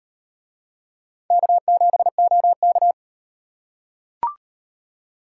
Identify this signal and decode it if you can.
Morse code, heard as a keyed tone, about 38 wpm, 700 Hz. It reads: K7OK